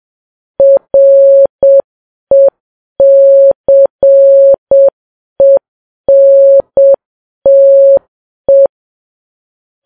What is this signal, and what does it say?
Morse code, heard as a keyed tone, about 7 wpm, 550 Hz. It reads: RECENTE
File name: Files Web tls / Morse on9fv / morse7kps